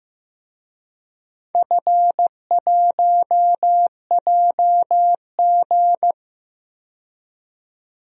Morse code, heard as a keyed tone, about 16 wpm, 700 Hz. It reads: F1JG
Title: DXCC-100 Call Signs - 4 Characters - 15wpm